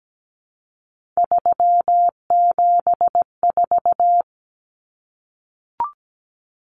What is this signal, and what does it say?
Morse code, heard as a keyed tone, about 17 wpm, 700 Hz. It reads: 374
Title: Numbers - 3-Digit 17wpm